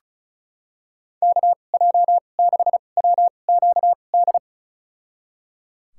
Morse code, heard as a keyed tone, about 36 wpm, spaced as for 25 wpm, 700 Hz. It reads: KJ6WQD
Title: USA Call Signs - 6 Characters - 2x Character Spacing - 35wpm